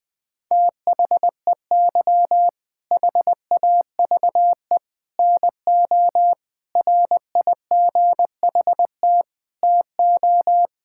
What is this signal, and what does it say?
Morse code, heard as a keyed tone, about 20 wpm, 700 Hz. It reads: THEY HAVE NO RIGHT TO
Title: Sentences from Top 1000 Words - Review Part 6 - 20wpm